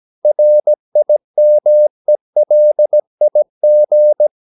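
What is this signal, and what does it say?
Morse code, heard as a keyed tone, about 17 wpm, 600 Hz. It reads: RIMELIG
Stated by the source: Slow (QRS) Morse practice at 17 wpm, from CW med Gnister.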